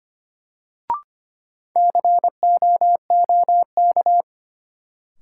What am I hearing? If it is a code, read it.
Morse code, heard as a keyed tone, about 25 wpm, 700 Hz. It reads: COOK
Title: Sentences from Top 700 Words - New Word Drill - 25wpm